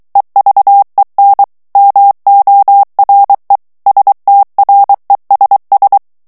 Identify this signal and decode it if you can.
Morse code, heard as a keyed tone, about 23 wpm, 800 Hz. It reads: EVEN MORE STRESS
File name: morse mp3 / ANGLAIS texte clair 1 vitesse 22 mots minute